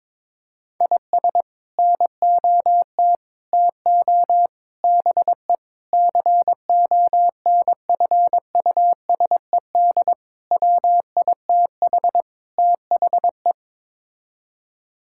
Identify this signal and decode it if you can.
Morse code, heard as a keyed tone, about 22 wpm, 700 Hz. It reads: IS NOT TO BE CONFUSED WITH THE
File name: practice / 22wpm / Sets of 7 Words - 22wpm